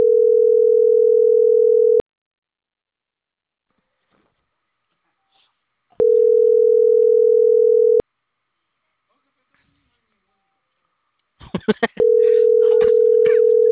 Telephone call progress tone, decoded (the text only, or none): ringback tone